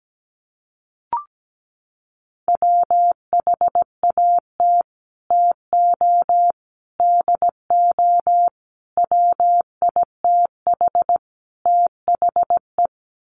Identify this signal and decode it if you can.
Morse code, heard as a keyed tone, about 17 wpm, 700 Hz. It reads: WHAT TO DO WITH THE
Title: Sets of 5 Words Encore2 17wpm